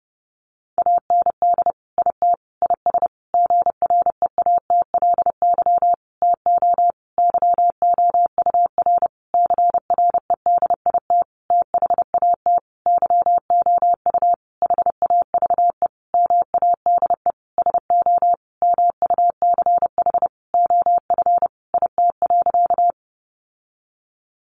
Morse code, heard as a keyed tone, about 30 wpm, 700 Hz. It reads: AND IT IS GREATLY TO YOUR CREDIT THAT YOU HAVE MADE SO MUCH OF IT.
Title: Sentences from Top 1400 Words 30wpm